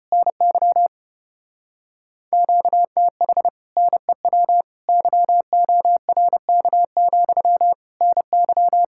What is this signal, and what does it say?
Morse code, heard as a keyed tone, about 33 wpm, 700 Hz. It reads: NY QTH NEW YORK, NY